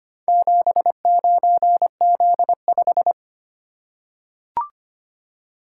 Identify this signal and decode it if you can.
Morse code, heard as a keyed tone, about 25 wpm, 700 Hz. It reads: 79Z5
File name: practice / 25wpm / Four Letter-Number Combinations - 25wpm